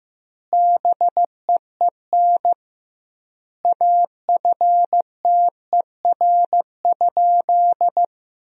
Morse code, heard as a keyed tone, about 15 wpm, 700 Hz. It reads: BEEN AFTER?
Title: 2x Word-Spacing - Sentences from Top 200 Words - 15wpm